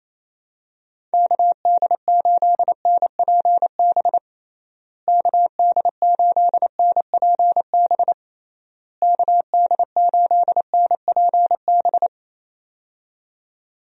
Morse code, heard as a keyed tone, about 28 wpm, 700 Hz. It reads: KD8NPB KD8NPB KD8NPB
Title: USA Call Signs - 6 Characters - Repeated 3x - 28wpm